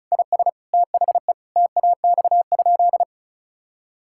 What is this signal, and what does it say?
Morse code, heard as a keyed tone, about 34 wpm, 700 Hz. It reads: IS THE TAX?
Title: Sentences from Top 1000 Words - Review Part 5 - 35wpm